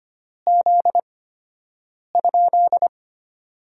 Morse code, heard as a keyed tone, about 25 wpm, spaced as for 7 wpm, 700 Hz. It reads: Z?
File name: Top 90 QSO Elements - 8x Character Spacing - 25wpm